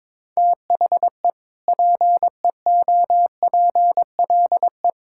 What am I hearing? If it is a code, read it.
Morse code, heard as a keyed tone, about 22 wpm, 700 Hz. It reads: THE PEOPLE
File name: practice / 22wpm / Sentences with 2-Word Phrases - 22wpm